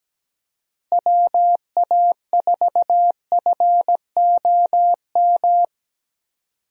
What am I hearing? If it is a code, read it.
Morse code, heard as a keyed tone, about 17 wpm, 700 Hz. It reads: WA4FOM
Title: USA Call Signs - 6 Characters - Repeated 3x - 17wpm